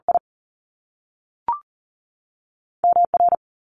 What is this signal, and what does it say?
Morse code, heard as a keyed tone, about 39 wpm, 700 Hz. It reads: I M R
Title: Top Two Letter Combinations in Words 40wpm